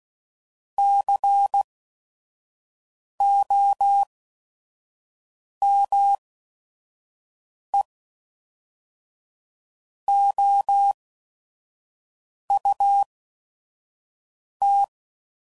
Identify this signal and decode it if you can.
Morse code, heard as a keyed tone, about 16 wpm, spaced as for 5 wpm, 800 Hz. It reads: COME OUT
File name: CW Code Groups @5WPM / QSOs / QSO WA7QOX 1